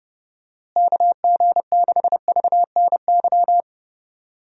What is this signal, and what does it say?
Morse code, heard as a keyed tone, about 30 wpm, 700 Hz. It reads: KG6VNY
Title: USA Call Signs - 6 Characters - Repeated 3x - 30wpm